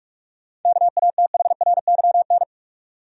Morse code, heard as a keyed tone, about 46 wpm, 700 Hz. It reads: KATHRYN
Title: USA Common Names - 45wpm